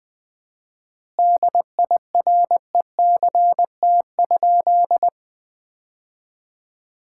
Morse code, heard as a keyed tone, about 20 wpm, 700 Hz. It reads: DIRECT?
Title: Common QSO Phrases - 4x Word-Spacing - 20wpm